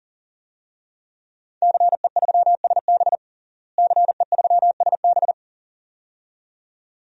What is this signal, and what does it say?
Morse code, heard as a keyed tone, about 40 wpm, 700 Hz. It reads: CE3SB CE3SB
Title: DXCC-100 Call Signs - 5 Characters - Repeated 2x - 40wpm